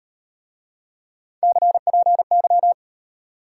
Morse code, heard as a keyed tone, about 34 wpm, 700 Hz. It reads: CPY